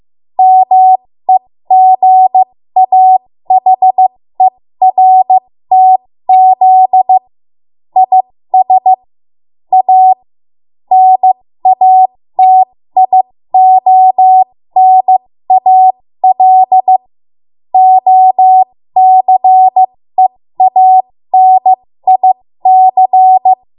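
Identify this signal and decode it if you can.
Morse code, heard as a keyed tone, about 15 wpm, spaced as for 13 wpm, 750 Hz. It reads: MEGAHERTZ IS A NATIONAL OCEANIC
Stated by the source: 13 wpm Morse code practice archive transmission.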